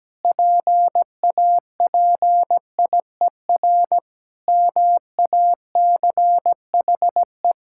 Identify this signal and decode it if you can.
Morse code, heard as a keyed tone, about 17 wpm, 700 Hz. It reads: PAPIER MACHE
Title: Pangram Challenge v2 - 17wpm